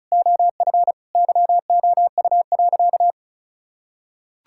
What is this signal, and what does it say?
Morse code, heard as a keyed tone, about 36 wpm, 700 Hz. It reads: OF YOU.
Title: Sentences from Top 1600 Words 35wpm